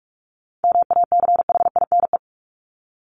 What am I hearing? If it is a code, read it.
Morse code, heard as a keyed tone, about 46 wpm, 700 Hz. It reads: MACHINE
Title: Top 500 Words Repeated 45wpm